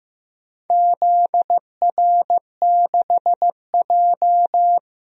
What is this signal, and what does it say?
Morse code, heard as a keyed tone, about 15 wpm, 700 Hz. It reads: ZR6J
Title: DXCC-100 Call Signs - 4 Characters - 15wpm